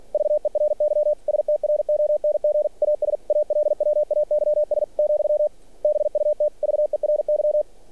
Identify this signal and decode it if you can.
Morse code, heard as a keyed tone, about 45 wpm, 600 Hz. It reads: VERY STRONG AS ALWAYS , BUT VERY
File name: archived / CRSnet